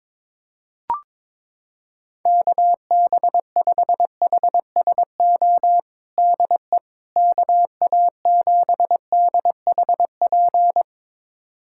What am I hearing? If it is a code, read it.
Morse code, heard as a keyed tone, about 22 wpm, 700 Hz. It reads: KB5HSO DE KA7DHP
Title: USA Callsign DE Callsign 22wpm